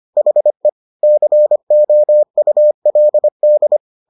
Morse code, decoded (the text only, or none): HE COULD